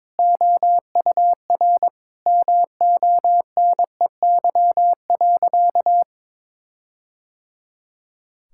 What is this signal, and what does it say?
Morse code, heard as a keyed tone, about 23 wpm, 700 Hz. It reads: OUR MONEY.
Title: Sentences from Top 1600 Words 22wpm